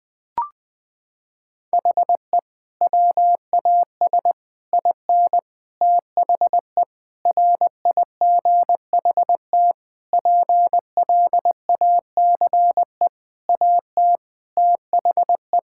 Morse code, heard as a keyed tone, about 19 wpm, 700 Hz. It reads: HE WAS IN THE RIGHT PLACE AT THE
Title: Sentences with 6-Word Phrases - 20wpm